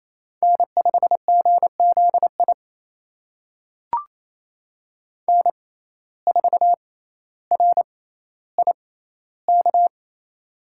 Morse code, heard as a keyed tone, about 28 wpm, spaced as for 14 wpm, 700 Hz. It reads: N5GZI N4RIK
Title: USA Call Signs - 5 Characters - 6x Character Spacing - 28wpm